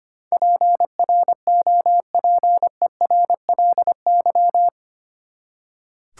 Morse code, heard as a keyed tone, about 25 wpm, 700 Hz. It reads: PROPERLY
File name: Eight Letter Words - 25wpm